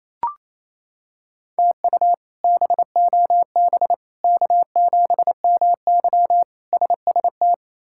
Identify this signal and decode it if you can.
Morse code, heard as a keyed tone, about 27 wpm, 700 Hz. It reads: TU BOB K7MY SST
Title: K1USN SST QSO Practice - 28wpm